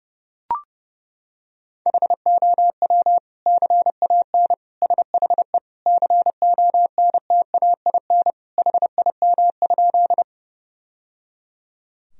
Morse code, heard as a keyed tone, about 30 wpm, 700 Hz. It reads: HOW CAN SHE CONTAIN HIM?
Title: Sentences from Top 1000 Words - Review Part 2 - 30wpm